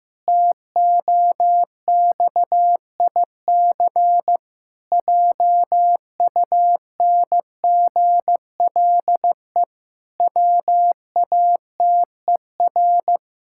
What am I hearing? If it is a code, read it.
Morse code, heard as a keyed tone, about 15 wpm, 700 Hz. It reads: TOXIC JUNGLE WATER
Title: Pangram Challenge v2 - 15wpm